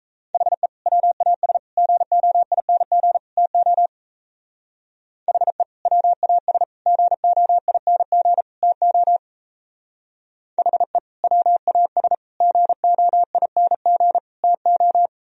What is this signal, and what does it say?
Morse code, heard as a keyed tone, about 35 wpm, 700 Hz. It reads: HE WAS GOING TO HE WAS GOING TO HE WAS GOING TO
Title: Speed-Racing - Sets of 4 Words 28wpm